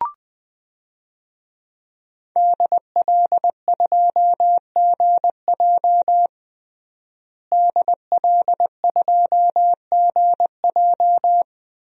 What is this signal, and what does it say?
Morse code, heard as a keyed tone, about 20 wpm, 700 Hz. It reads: DL2GJ DL2GJ
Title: DXCC-100 Call Signs - 5 Characters - Repeated 2x - 20wpm